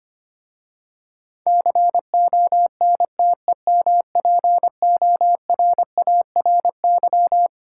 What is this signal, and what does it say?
Morse code, heard as a keyed tone, about 25 wpm, 700 Hz. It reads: CONTEMPORARY
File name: practice / 25wpm / Difficult Words with Hints - 25wpm